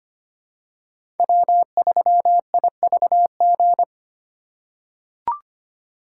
Morse code, heard as a keyed tone, about 25 wpm, 700 Hz. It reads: W3IVG
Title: USA Call Signs - 2x Character Spacing - 25wpm